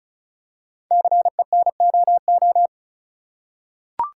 Morse code, heard as a keyed tone, about 36 wpm, 700 Hz. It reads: CENOO
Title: Code Groups - 5 Letters - 35wpm